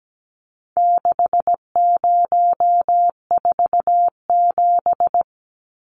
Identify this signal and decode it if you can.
Morse code, heard as a keyed tone, about 17 wpm, 700 Hz. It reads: 6047
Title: Numbers - 4-Digit 17wpm